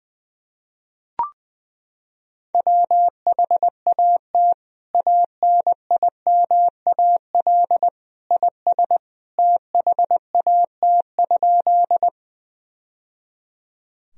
Morse code, heard as a keyed tone, about 20 wpm, 700 Hz. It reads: WHAT ANIMAL IS THAT?
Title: Sentences from Top 1000 Words - Review Part 1 - 20wpm